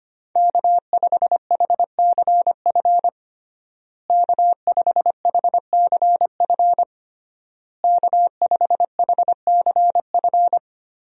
Morse code, heard as a keyed tone, about 25 wpm, 700 Hz. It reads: K5HCF K5HCF K5HCF
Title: USA Call Signs - 5 Characters - Repeated 3x - 25wpm